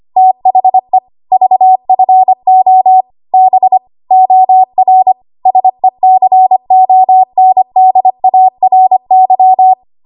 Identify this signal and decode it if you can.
Morse code, heard as a keyed tone, about 25 wpm, 750 Hz. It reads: THE VFO B OR SECONDARY
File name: Archive / 25 WPM / 25WPM